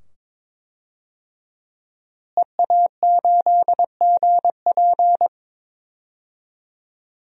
Morse code, heard as a keyed tone, about 21 wpm, 700 Hz. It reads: EA8GP
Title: DXCC-100 Call Signs - 5 Characters - Repeated 2x - 22wpm